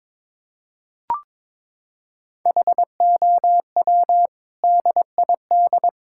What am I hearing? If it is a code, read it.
Morse code, heard as a keyed tone, about 22 wpm, 700 Hz. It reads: HOW DID
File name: practice / 22wpm / Sets of 7 Words - 22wpm